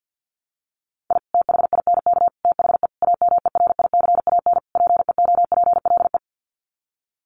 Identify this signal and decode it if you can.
Morse code, heard as a keyed tone, about 49 wpm, 700 Hz. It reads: I THINK THE AMERICAN PEOPLE
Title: Sets of 5 Words Encore 50wpm